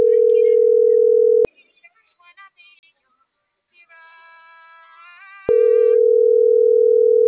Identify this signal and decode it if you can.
Ringback tone (North America), 440 + 480 Hz.